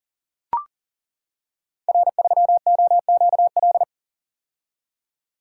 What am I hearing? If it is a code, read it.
Morse code, heard as a keyed tone, about 42 wpm, 700 Hz. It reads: R3OQL
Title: Code Groups - 5 Letter:Numbers - 40wpm